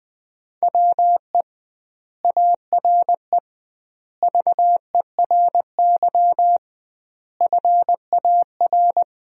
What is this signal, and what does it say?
Morse code, heard as a keyed tone, about 20 wpm, 700 Hz. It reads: WE ARE VERY FAR
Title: 2x Word-Spacing - Sentences from Top 300 Words - 20wpm